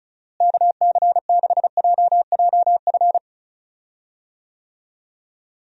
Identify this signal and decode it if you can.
Morse code, heard as a keyed tone, about 36 wpm, 700 Hz. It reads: KC6JJF
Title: USA Call Signs - 6 Characters - Repeated 2x - 35wpm